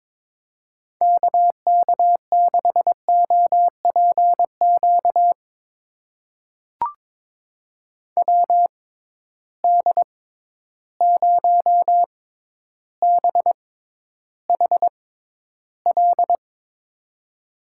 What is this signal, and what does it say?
Morse code, heard as a keyed tone, about 22 wpm, spaced as for 11 wpm, 700 Hz. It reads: KK6OPQ WD0BHL
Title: USA Call Signs - 6x Character Spacing - 22wpm